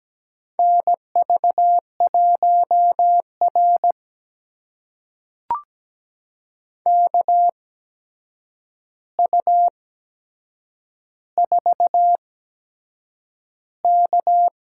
Morse code, heard as a keyed tone, about 17 wpm, spaced as for 7 wpm, 700 Hz. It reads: NV1R KU4K
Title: USA Call Signs - 4 Characters - 8x Character Spacing - 17wpm